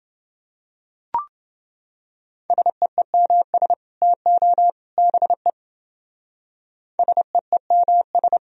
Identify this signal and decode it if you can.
Morse code, heard as a keyed tone, about 28 wpm, 700 Hz. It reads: SEEMS TO BE SEEMS